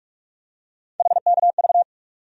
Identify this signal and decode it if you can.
Morse code, heard as a keyed tone, about 41 wpm, 700 Hz. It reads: HKV
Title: USA Call Sign Postfixes - 3 Letters 45wpm